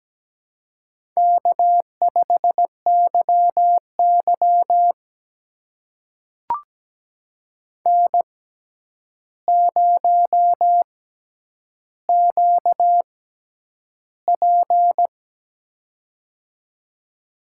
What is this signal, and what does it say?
Morse code, heard as a keyed tone, about 17 wpm, spaced as for 9 wpm, 700 Hz. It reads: K5YY N0QP
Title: USA Call Signs - 4 Characters - 6x Character Spacing - 17wpm